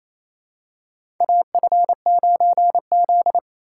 Morse code, heard as a keyed tone, about 27 wpm, 700 Hz. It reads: AF9Z
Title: USA Call Signs - 4 Characters - 3x Character Spacing - 28wpm